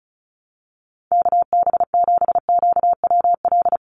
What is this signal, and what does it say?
Morse code, heard as a keyed tone, about 36 wpm, 700 Hz. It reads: KB7QWL
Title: USA Call Signs 35wpm